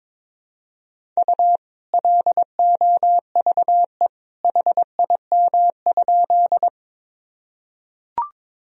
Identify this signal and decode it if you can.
Morse code, heard as a keyed tone, about 22 wpm, 700 Hz. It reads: U LOVE HIM?
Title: Sentences from Top 400 Words and Abbreviations 22wpm